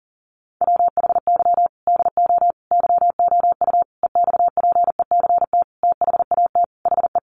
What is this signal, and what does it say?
Morse code, heard as a keyed tone, about 37 wpm, 700 Hz. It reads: WHY DO YOU EXPECT THAT HE
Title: Sentences from Top 1100 Words 40wpm